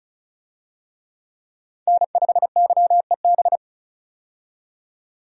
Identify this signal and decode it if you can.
Morse code, heard as a keyed tone, about 36 wpm, 700 Hz. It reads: N5YEB
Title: USA Call Signs - 5 Characters - 35wpm